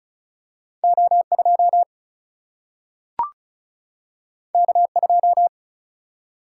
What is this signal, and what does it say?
Morse code, heard as a keyed tone, about 36 wpm, 700 Hz. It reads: O2 K2
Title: Single-Letter Single-Number 35wpm